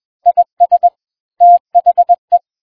Morse code, heard as a keyed tone, about 21 wpm, 700 Hz. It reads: IS THE